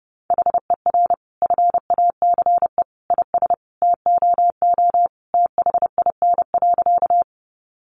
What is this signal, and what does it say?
Morse code, heard as a keyed tone, about 30 wpm, 700 Hz. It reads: HER FACE IS TOO THIN.
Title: Sentences from Top 1200 Words 30wpm